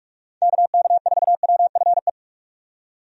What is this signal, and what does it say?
Morse code, heard as a keyed tone, about 46 wpm, 700 Hz. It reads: KK4WFE
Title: USA Call Signs - 6 Characters - 45wpm